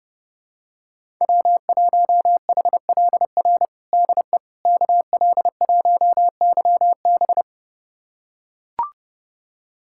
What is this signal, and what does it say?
Morse code, heard as a keyed tone, about 32 wpm, 700 Hz. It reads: W1HLR DE KL1YB
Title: USA Callsign DE Callsign - 5 Characters - 30wpm